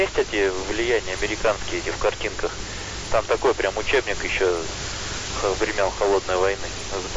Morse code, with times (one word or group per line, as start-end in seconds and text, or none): none